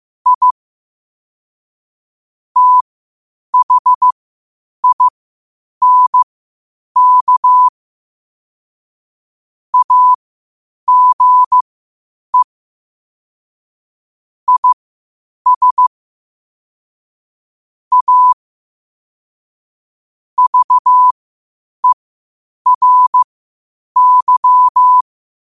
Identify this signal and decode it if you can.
Morse code, heard as a keyed tone, about 15 wpm, spaced as for 8 wpm, 1000 Hz. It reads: I THINK AGE IS A VERY
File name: qotdfiles / 8wpm